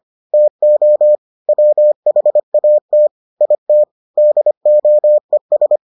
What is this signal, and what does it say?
Morse code, heard as a keyed tone, about 25 wpm, 600 Hz. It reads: TO WHAT IT DOES